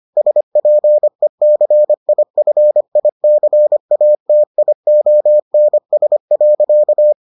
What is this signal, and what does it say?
Morse code, heard as a keyed tone, about 25 wpm, 600 Hz. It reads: SPECIFICATIONS.